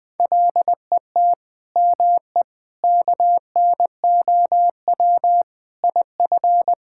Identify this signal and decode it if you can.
Morse code, heard as a keyed tone, about 20 wpm, 700 Hz. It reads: LET ME KNOW IF
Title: Sentences with 7-Word Phrases - 20wpm